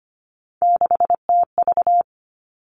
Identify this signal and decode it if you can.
Morse code, heard as a keyed tone, about 25 wpm, 700 Hz. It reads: 6TV
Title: Three Challenging Letter-Number Combinations 25wpm